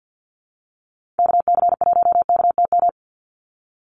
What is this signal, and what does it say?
Morse code, heard as a keyed tone, about 50 wpm, 700 Hz. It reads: KC1KTM
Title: USA Call Signs 50wpm